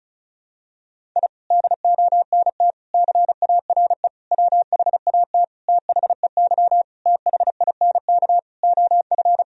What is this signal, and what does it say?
Morse code, heard as a keyed tone, about 36 wpm, 700 Hz. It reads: I DONT CARE WHAT THEY THINK OF
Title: Sentences from Top 1000 Words - Review Part 1 - 35wpm